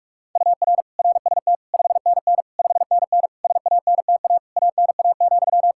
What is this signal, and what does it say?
Morse code, heard as a keyed tone, about 43 wpm, 700 Hz. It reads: UR RST 5NN 5NN SANTA ANA,
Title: Practice QSO - Exchange 1 - Beginning - 45wpm